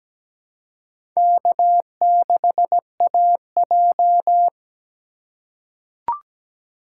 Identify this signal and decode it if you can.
Morse code, heard as a keyed tone, about 17 wpm, 700 Hz. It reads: K6AJ
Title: USA Call Signs - 2x Character Spacing - 17wpm